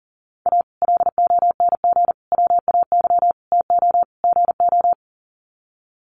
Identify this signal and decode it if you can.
Morse code, heard as a keyed tone, about 40 wpm, 700 Hz. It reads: A LONG WAY TO GO
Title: Sets of 5 Words 40wpm